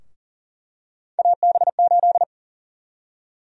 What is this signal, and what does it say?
Morse code, heard as a keyed tone, about 40 wpm, 700 Hz. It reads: AB8
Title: USA Call Sign Prefixes - Repeated 2x - 40wpm